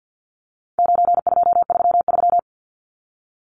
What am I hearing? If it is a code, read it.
Morse code, heard as a keyed tone, about 50 wpm, 700 Hz. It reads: 9233
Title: Numbers - 4-Digit 50wpm